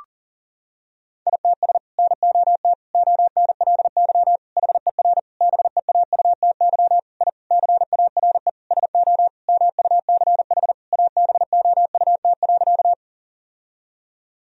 Morse code, heard as a keyed tone, about 40 wpm, 700 Hz. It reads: ITS NOT ONLY HER BEAUTY I CARE SO MUCH ABOUT.
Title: Sentences from Top 1000 Words - Review Part 2 - 40wpm